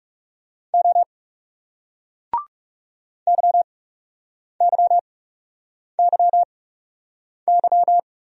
Morse code, heard as a keyed tone, about 40 wpm, spaced as for 8 wpm, 700 Hz. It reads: O YYYY